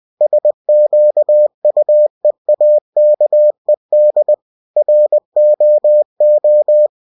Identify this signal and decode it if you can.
Morse code, heard as a keyed tone, about 20 wpm, 600 Hz. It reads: SQUEAKED ROO